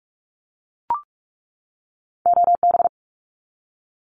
Morse code, heard as a keyed tone, about 44 wpm, 700 Hz. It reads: OB
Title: Abbreviations 45wpm